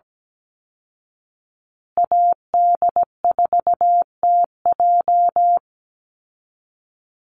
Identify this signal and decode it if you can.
Morse code, heard as a keyed tone, about 17 wpm, 700 Hz. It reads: AD4TJ
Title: Call Signs Repeated 3x - 17wpm